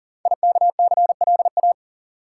Morse code, heard as a keyed tone, about 39 wpm, 700 Hz. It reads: IKCLA
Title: Code Groups - 5 Letter:Numbers - 40wpm